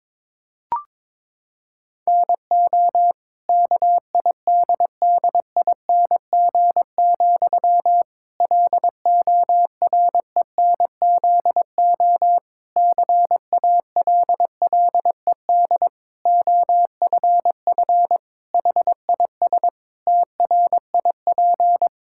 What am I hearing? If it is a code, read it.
Morse code, heard as a keyed tone, about 22 wpm, 700 Hz. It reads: NO KIDDING, LORENZO CALLED OFF HIS TRIP
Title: Pangram Challenge v2 - 22wpm